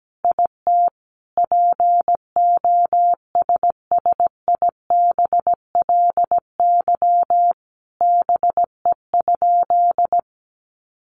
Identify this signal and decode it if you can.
Morse code, heard as a keyed tone, about 17 wpm, 700 Hz. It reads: IT POSSIBLY BE?
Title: Sentences from Top 1300 Words 17wpm